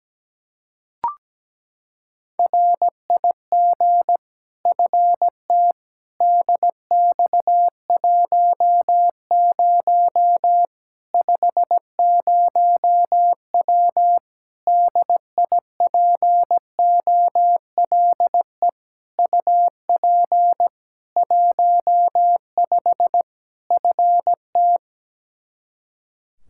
Morse code, heard as a keyed tone, about 17 wpm, 700 Hz. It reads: RIG FT DX10 50W DIPOLE UP 15 FT
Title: K1USN SST QSO Practice 17wpm